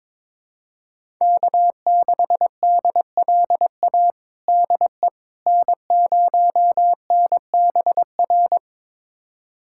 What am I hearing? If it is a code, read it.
Morse code, heard as a keyed tone, about 22 wpm, 700 Hz. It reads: K6DLA DE N0NBR
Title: USA Callsign DE Callsign - 5 Characters - 22wpm